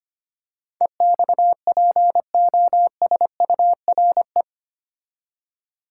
Morse code, heard as a keyed tone, about 25 wpm, 690 Hz. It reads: EXPOSURE
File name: 8 Letter Words - First 3 Letters Given - 25wpm